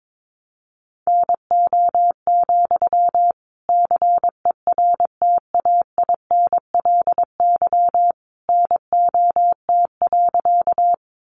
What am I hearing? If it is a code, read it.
Morse code, heard as a keyed tone, about 22 wpm, 700 Hz. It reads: NO, CERTAINLY NOT.
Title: Sentences from Top 900 Words 22wpm